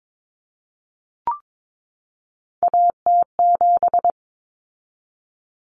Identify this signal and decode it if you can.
Morse code, heard as a keyed tone, about 22 wpm, 700 Hz. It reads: AT7
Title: Common Cut Numbers - 3-Digit 22wpm